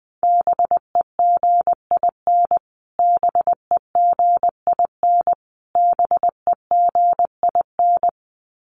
Morse code, heard as a keyed tone, about 20 wpm, 700 Hz. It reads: BEGIN BEGIN BEGIN
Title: Top 300 Words Repeated 20wpm